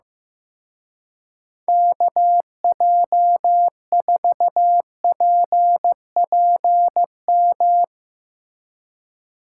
Morse code, heard as a keyed tone, about 16 wpm, 700 Hz. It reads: KJ4PPM
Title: Speed Racing - USA Call Signs 15wpm